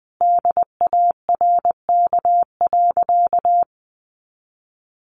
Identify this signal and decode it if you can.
Morse code, heard as a keyed tone, about 20 wpm, 700 Hz. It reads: DARK.
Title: Sentences from Top 1400 Words 20wpm